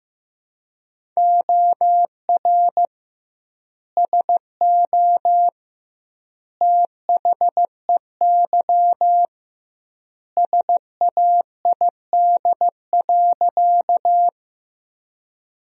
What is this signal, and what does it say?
Morse code, heard as a keyed tone, about 16 wpm, 700 Hz. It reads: OR SO THEY SAID.
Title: 2x Word-Spacing - Sentences from Top 100 Words - 15wpm